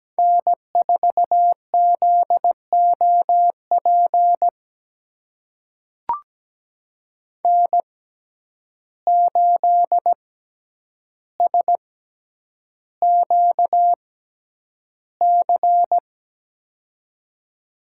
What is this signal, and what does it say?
Morse code, heard as a keyed tone, about 17 wpm, spaced as for 9 wpm, 700 Hz. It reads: N4ZOP N8SQC
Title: USA Call Signs - 5 Characters - 6x Character Spacing - 17wpm